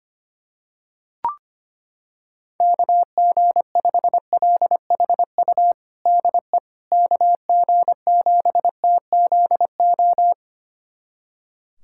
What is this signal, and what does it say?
Morse code, heard as a keyed tone, about 25 wpm, 700 Hz. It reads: KG5LHU DE KG7TZO
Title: USA Callsign DE Callsign - 6 Characters - 25wpm